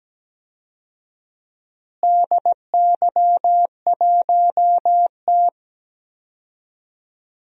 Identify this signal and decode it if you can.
Morse code, heard as a keyed tone, about 17 wpm, 700 Hz. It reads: DY1T
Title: DXCC-100 Call Signs - 4 Characters - Repeated 3x - 17wpm